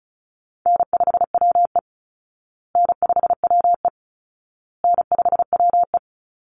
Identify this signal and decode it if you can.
Morse code, heard as a keyed tone, about 34 wpm, 700 Hz. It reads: N5WE N5WE N5WE
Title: Call Signs Repeated 2x - 35wpm